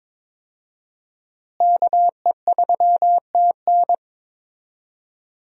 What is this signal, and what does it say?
Morse code, heard as a keyed tone, about 22 wpm, 700 Hz. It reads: KE3TN
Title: USA Call Signs - 5 Characters - Repeated 3x - 22wpm